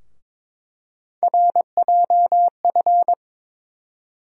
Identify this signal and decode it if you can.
Morse code, heard as a keyed tone, about 22 wpm, 700 Hz. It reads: RJF
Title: USA Call Sign Postfixes - 3 Letters 22wpm